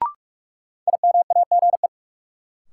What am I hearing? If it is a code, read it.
Morse code, heard as a keyed tone, about 44 wpm, 700 Hz. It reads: IMAGE
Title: Sentences from Top 700 Words - New Word Drill - 45wpm